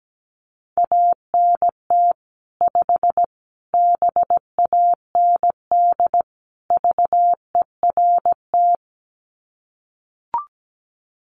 Morse code, heard as a keyed tone, about 17 wpm, 700 Hz. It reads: ANT 5 BAND VERT
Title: Common QSO Phrases 17wpm